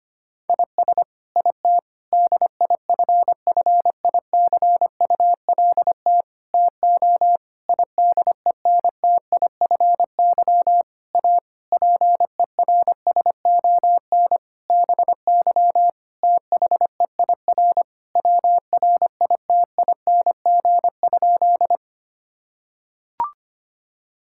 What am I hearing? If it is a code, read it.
Morse code, heard as a keyed tone, about 25 wpm, 700 Hz. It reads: IS IT DIFFICULT TO IDENTIFY A PERSON BY THEIR WRITING?
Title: Sentences from Top 1000 Words - Review Part 6 - 25wpm